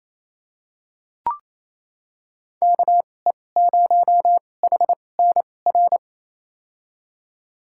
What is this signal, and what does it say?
Morse code, heard as a keyed tone, about 28 wpm, spaced as for 20 wpm, 700 Hz. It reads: KE0HNR
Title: USA Call Signs - 2x Character Spacing - 28wpm